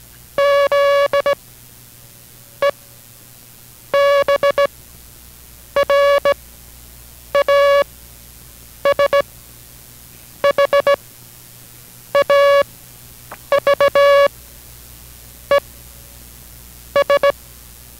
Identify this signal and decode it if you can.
Morse code, heard as a keyed tone, about 17 wpm, spaced as for 6 wpm, 560 Hz. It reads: ZEBRAS HAVE S